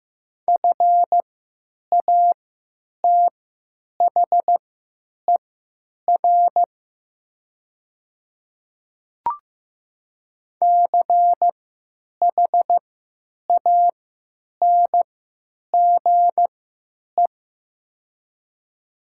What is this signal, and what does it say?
Morse code, heard as a keyed tone, about 15 wpm, spaced as for 9 wpm, 700 Hz. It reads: FATHER CHANGE
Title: 3x Character Spacing - Top 200 Words - 15wpm